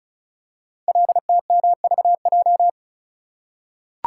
Morse code, heard as a keyed tone, about 36 wpm, 690 Hz. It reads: LTMVJ
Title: Code Groups - 5 Letters - 35wpm